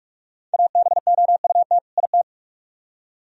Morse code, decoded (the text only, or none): ABOUT IT